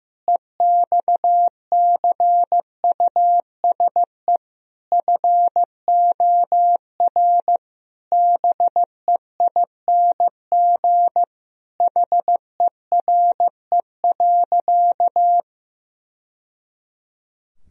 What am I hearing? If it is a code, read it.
Morse code, heard as a keyed tone, about 15 wpm, 700 Hz. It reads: EXCUSE FOR BEING HERE.
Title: Sentences from Top 1000 Words - Review Part 3 - 15wpm